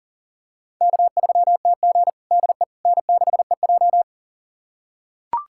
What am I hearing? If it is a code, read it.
Morse code, heard as a keyed tone, about 40 wpm, 700 Hz. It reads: K3TG DE N6EJ
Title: USA Callsign DE Callsign - 4 Characters - 40wpm